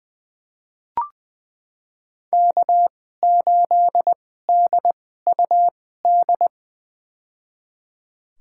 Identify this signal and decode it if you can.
Morse code, heard as a keyed tone, about 20 wpm, spaced as for 14 wpm, 700 Hz. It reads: K8DUD
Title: USA Call Signs - 5 Characters - 2x Character Spacing - 20wpm